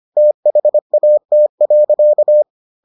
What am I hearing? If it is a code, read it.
Morse code, heard as a keyed tone, about 25 wpm, 600 Hz. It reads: THAT.